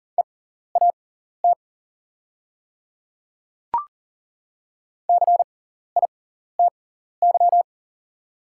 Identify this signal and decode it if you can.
Morse code, heard as a keyed tone, about 40 wpm, spaced as for 14 wpm, 690 Hz. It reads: EAT CITY
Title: 6x Character Spacing - Top 300 Words - 40wpm